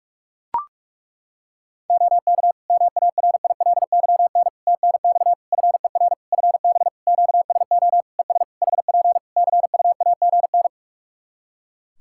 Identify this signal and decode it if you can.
Morse code, heard as a keyed tone, about 42 wpm, 700 Hz. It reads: OK MARILYN TNX FER FB QSO ES HP CUAGN
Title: Practice QSO - Exchange 4 - Ending - 45wpm